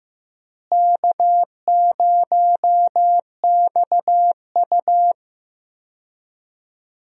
Morse code, heard as a keyed tone, about 16 wpm, 700 Hz. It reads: K0XU
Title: USA Call Signs - 4 Characters - Repeated 2x - 15wpm